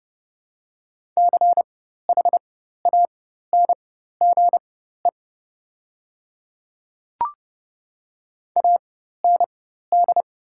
Morse code, heard as a keyed tone, about 30 wpm, spaced as for 14 wpm, 700 Hz. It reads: CHANGE AND
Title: 4x Character Spacing - Top 200 Words - 30wpm